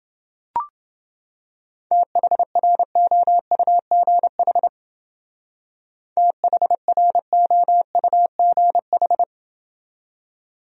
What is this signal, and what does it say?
Morse code, heard as a keyed tone, about 29 wpm, 700 Hz. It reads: THROUGH THROUGH